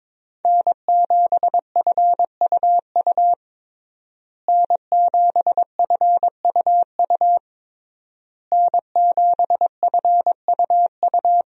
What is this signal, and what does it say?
Morse code, heard as a keyed tone, about 22 wpm, 700 Hz. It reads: N7FUU N7FUU N7FUU
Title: USA Call Signs - 5 Characters - Repeated 3x - 22wpm